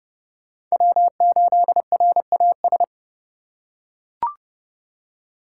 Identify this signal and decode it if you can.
Morse code, heard as a keyed tone, about 30 wpm, 700 Hz. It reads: W8RAS
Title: USA Call Signs - 5 Characters - 4x Character Spacing - 30wpm